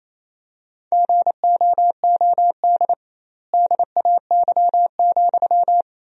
Morse code, heard as a keyed tone, about 28 wpm, 700 Hz. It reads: GOOD DAY,
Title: 2x Word-Spacing - Sentences from Top 200 Words - 28wpm